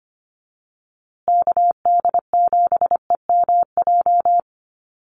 Morse code, heard as a keyed tone, about 25 wpm, 700 Hz. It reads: KD7EMJ
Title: USA Call Signs Repeated 3x 25wpm